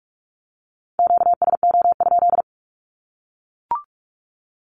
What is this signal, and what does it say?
Morse code, heard as a keyed tone, about 44 wpm, 700 Hz. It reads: QSO?
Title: Q-Codes 45wpm